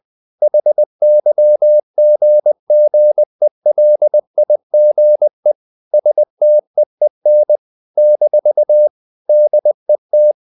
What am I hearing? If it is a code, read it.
Morse code, heard as a keyed tone, about 20 wpm, 600 Hz. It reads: HYGGELIGE STEEN - DET
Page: Ferie 20 wpm